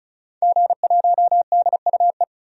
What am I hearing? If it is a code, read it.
Morse code, heard as a keyed tone, about 35 wpm, 700 Hz. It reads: G1DUE
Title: DXCC-100 Call Signs - 5 Characters - 35wpm